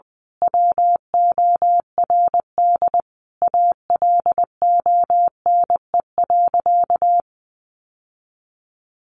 Morse code, heard as a keyed tone, about 20 wpm, 700 Hz. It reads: WORD ALONE.
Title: Sentences from Top 1000 Words 20wpm